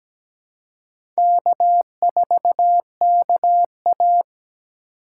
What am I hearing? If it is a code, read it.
Morse code, heard as a keyed tone, about 17 wpm, 700 Hz. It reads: K4KA
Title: USA Call Signs - 4 Characters - 3x Character Spacing - 17wpm